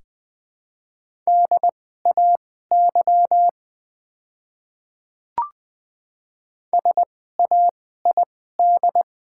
Morse code, heard as a keyed tone, about 20 wpm, spaced as for 14 wpm, 700 Hz. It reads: DAY SAID
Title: 2x Character Spacing - Top 200 Words - 20wpm